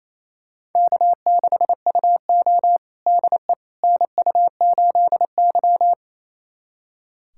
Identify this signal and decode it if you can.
Morse code, heard as a keyed tone, about 27 wpm, 690 Hz. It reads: K6UO DE NU8Y
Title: USA Callsign DE Callsign - 4 Characters - 28wpm